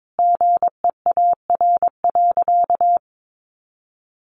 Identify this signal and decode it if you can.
Morse code, heard as a keyed tone, about 22 wpm, 700 Hz. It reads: GEAR.
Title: Sentences from Top 1400 Words 22wpm